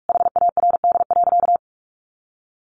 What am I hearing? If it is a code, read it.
Morse code, heard as a keyed tone, about 44 wpm, 700 Hz. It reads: HARD.
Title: Sentences from Top 1200 Words 45wpm